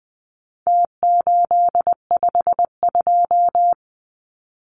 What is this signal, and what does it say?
Morse code, heard as a keyed tone, about 20 wpm, 700 Hz. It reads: T852
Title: Numbers - Contest Style - 4-Digit 20wpm